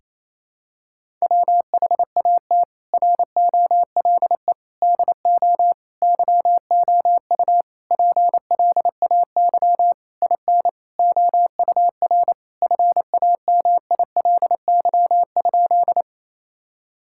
Morse code, heard as a keyed tone, about 28 wpm, 700 Hz. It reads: WHAT ROLE DO YOU PLAY IN OUR FAMILY?
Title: Sentences from Top 1000 Words - Review Part 6 - 28wpm